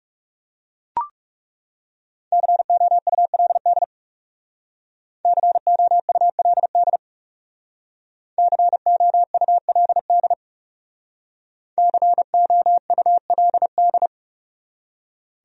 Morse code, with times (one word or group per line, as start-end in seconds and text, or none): 2.3-3.8 COULD
5.2-7.0 COULD
8.4-10.3 COULD
11.8-14.1 COULD